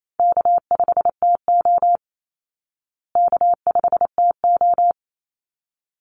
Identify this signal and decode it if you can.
Morse code, heard as a keyed tone, about 28 wpm, 700 Hz. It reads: K5TO K5TO
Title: Call Signs Repeated 3x - 28wpm